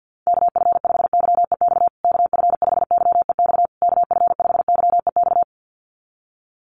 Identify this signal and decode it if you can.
Morse code, heard as a keyed tone, about 50 wpm, 700 Hz. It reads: KF5YEX KF5YEX KF5YEX
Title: USA Call Signs Repeated 3x 50wpm